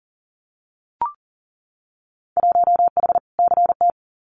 Morse code, heard as a keyed tone, about 40 wpm, 700 Hz. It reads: 1H CT
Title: Winter Field Day Exchanges 40wpm